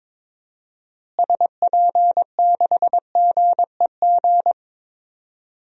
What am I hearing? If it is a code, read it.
Morse code, heard as a keyed tone, about 22 wpm, 700 Hz. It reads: SP6GEG
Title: DXCC-100 Call Signs - 6 Characters - Repeated 3x - 22wpm